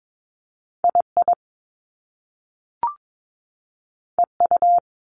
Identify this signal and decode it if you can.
Morse code, heard as a keyed tone, about 22 wpm, 700 Hz. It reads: II EU
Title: Top Two Letter Combinations in Words 22wpm